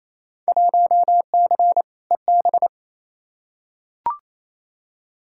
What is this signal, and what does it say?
Morse code, heard as a keyed tone, about 28 wpm, 700 Hz. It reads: 1C EB
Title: Field Day Exchanges Encore - 28wpm